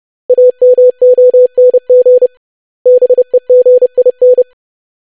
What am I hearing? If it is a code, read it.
Morse code, heard as a keyed tone, about 30 wpm, 500 Hz. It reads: AMONG BEGIN
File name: lesson / web / basic500-25wpm